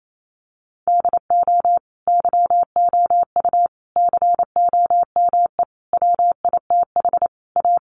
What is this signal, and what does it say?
Morse code, heard as a keyed tone, about 28 wpm, 700 Hz. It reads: DO YOU COME WITH A
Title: Sentences from Top 1200 Words 28wpm